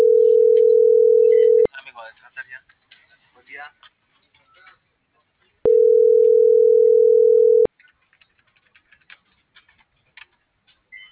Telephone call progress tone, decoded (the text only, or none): ringback tone